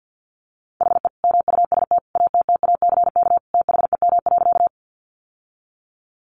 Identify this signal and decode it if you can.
Morse code, heard as a keyed tone, about 50 wpm, 700 Hz. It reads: HE MUST ATTACK THEM.
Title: Sentences from Top 1500 Words 50wpm